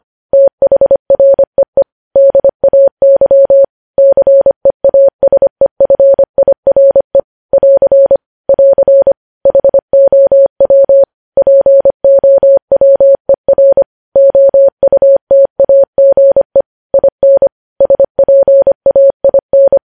Morse code, heard as a keyed tone, about 25 wpm, 550 Hz. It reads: THREE DAY CEASEFIRE <AR> <AR> HOW POWER OUTAGE IN SPAIN
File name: podcast / News headlines-1x-25x25